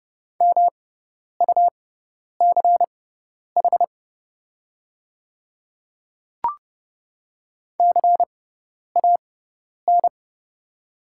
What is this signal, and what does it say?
Morse code, heard as a keyed tone, about 30 wpm, spaced as for 10 wpm, 690 Hz. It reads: MUCH CAN
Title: 6x Character Spacing - Top 300 Words - 30wpm